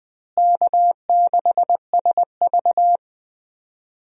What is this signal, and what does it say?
Morse code, heard as a keyed tone, about 21 wpm, 700 Hz. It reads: K6SV
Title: USA Call Signs - 4 Characters - Repeated 2x - 20wpm